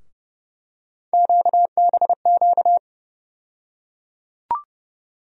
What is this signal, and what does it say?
Morse code, heard as a keyed tone, about 30 wpm, 700 Hz. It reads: QBQ
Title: USA Call Sign Postfixes - 3 Letters 30wpm